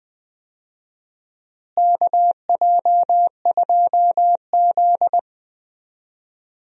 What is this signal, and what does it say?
Morse code, heard as a keyed tone, about 20 wpm, 700 Hz. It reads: KJ2Z
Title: USA Call Signs - 4 Characters - Repeated 3x - 20wpm